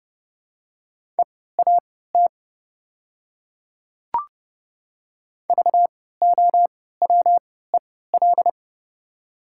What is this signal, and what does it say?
Morse code, heard as a keyed tone, about 30 wpm, spaced as for 17 wpm, 690 Hz. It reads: EAT VOWEL
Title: 3x Character Spacing - Top 500 Words - 30wpm